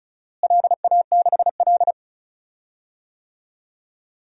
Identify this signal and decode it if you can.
Morse code, heard as a keyed tone, about 36 wpm, 700 Hz. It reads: LA6L
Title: DXCC-100 Call Signs - 4 Characters - Repeated 3x - 35wpm